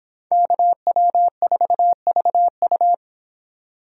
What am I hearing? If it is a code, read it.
Morse code, heard as a keyed tone, about 24 wpm, 700 Hz. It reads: KW4VU